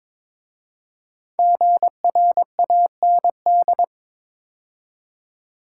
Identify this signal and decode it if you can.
Morse code, heard as a keyed tone, about 22 wpm, 700 Hz. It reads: GRAND
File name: 5 Letter Words - First 3 Letters Given - 22wpm